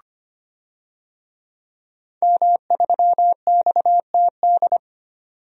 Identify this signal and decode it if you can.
Morse code, heard as a keyed tone, about 25 wpm, 700 Hz. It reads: M3XTD
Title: DXCC-100 Call Signs - 5 Characters - 25wpm